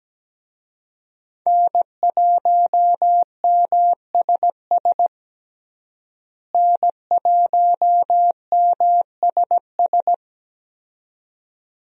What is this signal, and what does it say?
Morse code, heard as a keyed tone, about 17 wpm, 700 Hz. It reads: N1MSS N1MSS
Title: USA Call Signs - 5 Characters - Repeated 2x - 17wpm